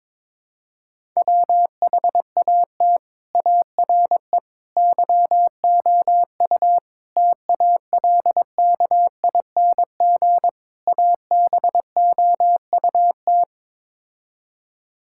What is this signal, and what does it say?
Morse code, heard as a keyed tone, about 21 wpm, 700 Hz. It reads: WHAT ARE YOU TALKING ABOUT
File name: practice / 22wpm / Build A Phrase - 5-Word Phrases - 22wpm